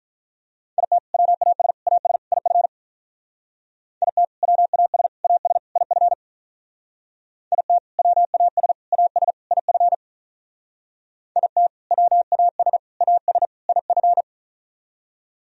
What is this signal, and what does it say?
Morse code, heard as a keyed tone, about 45 wpm, 700 Hz. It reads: IT WAS AS IF IT WAS AS IF IT WAS AS IF IT WAS AS IF